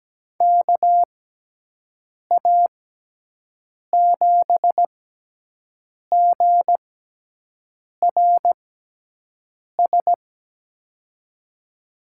Morse code, heard as a keyed tone, about 17 wpm, spaced as for 6 wpm, 700 Hz. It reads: KA7GRS
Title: USA Call Signs - 6x Character Spacing - 17wpm